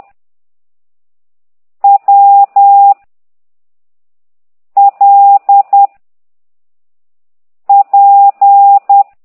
Morse code, heard as a keyed tone, about 10 wpm, 800 Hz. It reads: W L P
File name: ct2idw / exerc9